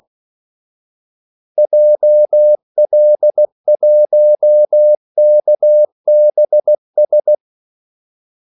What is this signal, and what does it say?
Morse code, heard as a keyed tone, about 16 wpm, 600 Hz. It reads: JL1KBS
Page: Kaldesignaler 16 wpm